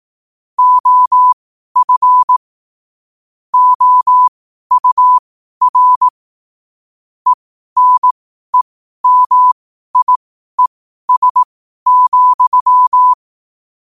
Morse code, heard as a keyed tone, about 18 wpm, spaced as for 12 wpm, 1000 Hz. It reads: OF OUR ENEMIES,
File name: qotdfiles / 12wpm